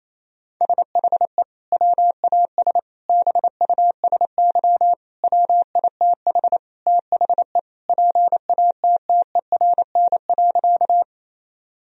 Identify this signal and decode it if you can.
Morse code, heard as a keyed tone, about 28 wpm, 700 Hz. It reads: SHE WAS BUSY WITH THE PATTERN.
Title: Sentences from Top 1000 Words - Review Part 2 - 28wpm